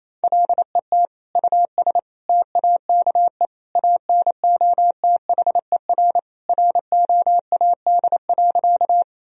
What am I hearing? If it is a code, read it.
Morse code, heard as a keyed tone, about 28 wpm, 700 Hz. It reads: LET US TAKE ANOTHER ROAD.
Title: Sentences from Top 1000 Words - Review Part 2 - 28wpm